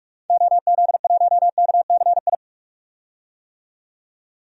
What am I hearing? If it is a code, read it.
Morse code, heard as a keyed tone, about 41 wpm, 700 Hz. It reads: OZ1KCI
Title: DXCC-100 Call Signs - 6 Characters - 45wpm